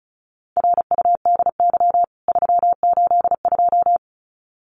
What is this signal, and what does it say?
Morse code, heard as a keyed tone, about 34 wpm, 700 Hz. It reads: RUDY 382
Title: CWT Exchange 35wpm